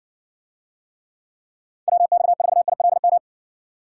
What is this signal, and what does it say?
Morse code, heard as a keyed tone, about 60 wpm, 700 Hz. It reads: WX4ILM